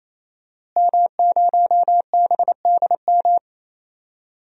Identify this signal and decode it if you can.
Morse code, heard as a keyed tone, about 28 wpm, 700 Hz. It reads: M0BDM
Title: DXCC-100 Call Signs - 5 Characters - Repeated 3x - 28wpm